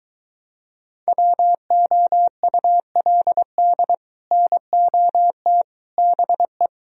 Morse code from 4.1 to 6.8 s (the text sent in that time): NOT BE